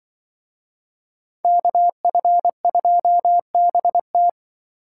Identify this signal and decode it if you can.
Morse code, heard as a keyed tone, about 24 wpm, 700 Hz. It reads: KF2BT
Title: Speed Racing - USA 2x2 Call Signs 20wpm